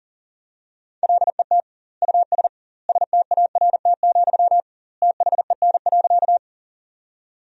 Morse code, heard as a keyed tone, about 40 wpm, 700 Hz. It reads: LET US START, THEN.
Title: 2x Word-Spacing - Sentences from Top 300 Words - 40wpm